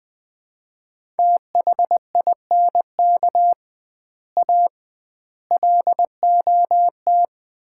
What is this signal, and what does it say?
Morse code, heard as a keyed tone, about 20 wpm, 700 Hz. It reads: THINK A LOT
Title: 2x Word-Spacing - Sets of 5 Words - 20wpm